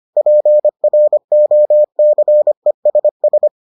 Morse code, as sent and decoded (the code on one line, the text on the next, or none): .--. .-. --- -.-. ... ...
PROCESS